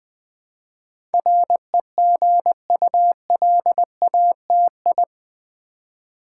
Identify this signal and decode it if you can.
Morse code, heard as a keyed tone, about 20 wpm, 700 Hz. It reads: REGULATI
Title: Build A Word - 4 Letters At A Time - 10-Letter Words - 20wpm